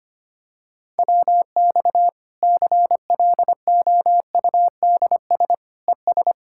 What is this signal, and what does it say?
Morse code, heard as a keyed tone, about 25 wpm, 700 Hz. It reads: WX CLOUDS ES
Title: Practice QSO - Exchange 2 - Info - 25wpm